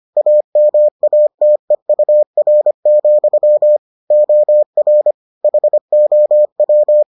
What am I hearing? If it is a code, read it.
Morse code, heard as a keyed tone, about 25 wpm, 600 Hz. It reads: AMATEUR, OR HOW